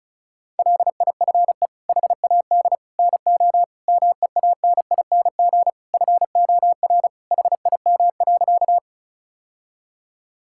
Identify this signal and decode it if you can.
Morse code, heard as a keyed tone, about 36 wpm, 700 Hz. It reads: LIFE HAD NO MEANING FOR HIM.
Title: Sentences from Top 1000 Words - Review Part 6 - 35wpm